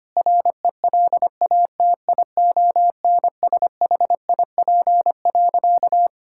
Morse code, heard as a keyed tone, about 25 wpm, 700 Hz. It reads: RELATIONSHIP.
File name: practice / 25wpm / Difficult Sentences with Hints - 25wpm